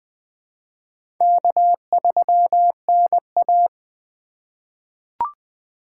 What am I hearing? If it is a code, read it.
Morse code, heard as a keyed tone, about 20 wpm, 700 Hz. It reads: K3NA
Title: USA Call Signs - 4 Characters - 3x Character Spacing - 20wpm